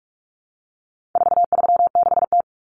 Morse code, heard as a keyed tone, about 44 wpm, 700 Hz. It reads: 436T
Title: Common Cut Numbers - 4-Digit 45wpm